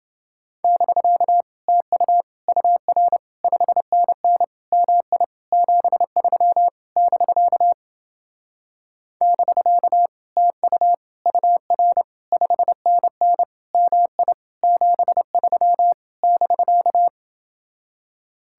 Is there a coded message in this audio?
Yes: Morse code from 0.6 to 17.1 s, heard as a keyed tone, about 29 wpm, 700 Hz. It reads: <BK> TU UR 5NN MI 73 <BK> <BK> TU UR 5NN MI 73 <BK>